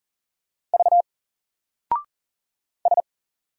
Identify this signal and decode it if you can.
Morse code, heard as a keyed tone, about 42 wpm, 700 Hz. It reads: V S
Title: Single Letter-Number Alt 40wpm